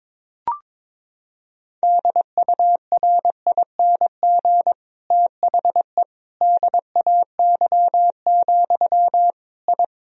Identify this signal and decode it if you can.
Morse code, heard as a keyed tone, about 22 wpm, 700 Hz. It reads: DURING THE DAY, I
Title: Sentences from Top 1000 Words - Review Part 2 - 22wpm